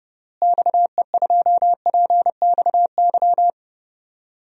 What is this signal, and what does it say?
Morse code, heard as a keyed tone, about 32 wpm, 700 Hz. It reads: XE2PXY
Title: DXCC-100 Call Signs - 6 Characters - Repeated 3x - 30wpm